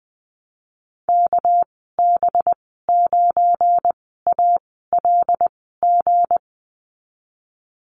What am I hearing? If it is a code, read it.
Morse code, heard as a keyed tone, about 20 wpm, spaced as for 15 wpm, 700 Hz. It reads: KB9ALG
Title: 231 Lesson 37g - Letter x - Call Signs 20wpm